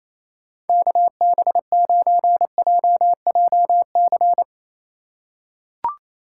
Morse code, heard as a keyed tone, about 28 wpm, 700 Hz. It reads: KB9JJC
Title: USA Call Signs - 6 Characters - 3x Character Spacing - 28wpm